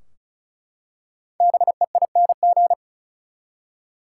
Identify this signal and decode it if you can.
Morse code, heard as a keyed tone, about 36 wpm, 700 Hz. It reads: BEING
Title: Five Letter Words - 35wpm